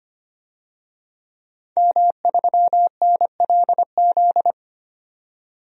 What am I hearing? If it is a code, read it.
Morse code, heard as a keyed tone, about 25 wpm, 690 Hz. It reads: M3NLZ
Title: DXCC-100 Call Signs - 5 Characters - 25wpm